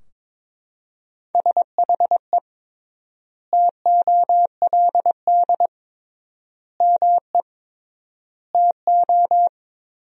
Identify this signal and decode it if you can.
Morse code, heard as a keyed tone, about 21 wpm, 700 Hz. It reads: SHE TOLD ME TO
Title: 3x Word-Spacing - Sentences from Top 400 Words - 22wpm